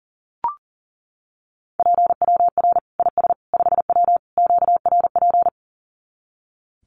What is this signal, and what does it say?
Morse code, heard as a keyed tone, about 40 wpm, 700 Hz. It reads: PWR IS 5W QRP
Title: Common QSO Phrases 40wpm